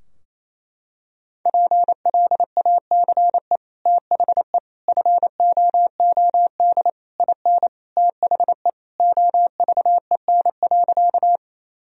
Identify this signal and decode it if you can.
Morse code, heard as a keyed tone, about 28 wpm, 700 Hz. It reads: PLACE THE FOOD IN THE OVEN.
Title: Sentences from Top 1000 Words - Review Part 4 - 28wpm